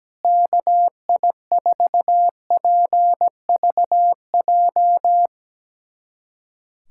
Morse code, heard as a keyed tone, about 17 wpm, 700 Hz. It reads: KI4PVJ
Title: USA Callsign DE Callsign - 6 Characters - 17wpm